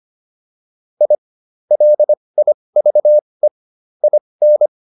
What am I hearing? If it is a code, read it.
Morse code, heard as a keyed tone, about 25 wpm, spaced as for 20 wpm, 600 Hz. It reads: I LIVE IN
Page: intermediate-practice-files